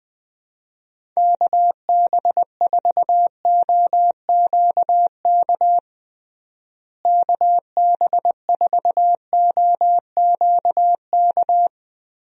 Morse code, heard as a keyed tone, about 20 wpm, 700 Hz. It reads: KB4OQK KB4OQK
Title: USA Call Signs - 6 Characters - Repeated 3x - 20wpm